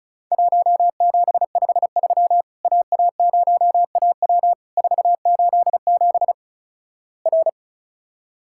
Morse code, heard as a keyed tone, about 36 wpm, 700 Hz. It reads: 1753 AA0AW 487 R
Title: Worked All Europe DX Contest - QTC Exchange - 35wpm